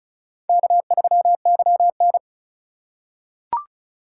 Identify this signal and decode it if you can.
Morse code, heard as a keyed tone, about 34 wpm, 700 Hz. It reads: K3YN
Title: USA Call Signs - 3x Character Spacing - 35wpm